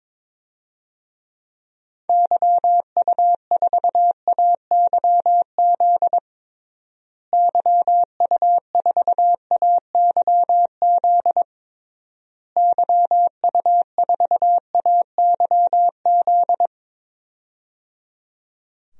Morse code, heard as a keyed tone, about 22 wpm, 700 Hz. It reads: YU4AYZ YU4AYZ YU4AYZ
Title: DXCC-100 Call Signs - 6 Characters - Repeated 3x - 22wpm